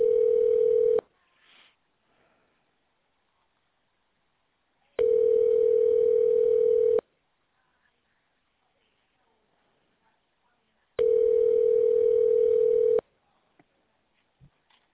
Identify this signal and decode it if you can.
Ringback tone (North America), 440 + 480 Hz.